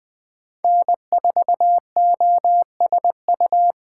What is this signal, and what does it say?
Morse code, heard as a keyed tone, about 20 wpm, 700 Hz. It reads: N4OSU
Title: USA Call Signs - 5 Characters - 3x Character Spacing - 20wpm